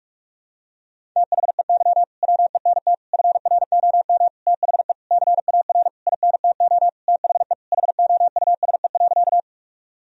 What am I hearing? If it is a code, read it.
Morse code, heard as a keyed tone, about 46 wpm, 700 Hz. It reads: THEY WENT FROM THE CAR INTO THE HOUSE.
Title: Sentences from Top 1000 Words - Review Part 1 - 45wpm